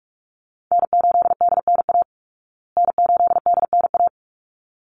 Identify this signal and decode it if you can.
Morse code, heard as a keyed tone, about 44 wpm, 700 Hz. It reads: N8DNA N8DNA
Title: Call Signs Repeated 3x - 45wpm